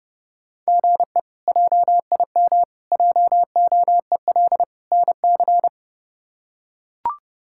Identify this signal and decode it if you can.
Morse code, heard as a keyed tone, about 30 wpm, 700 Hz. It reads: GE JIM JOEL NC
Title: K1USN SST QSO Practice - 30wpm